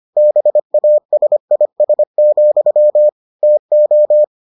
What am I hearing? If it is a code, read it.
Morse code, heard as a keyed tone, about 25 wpm, 600 Hz. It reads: BASIS, TO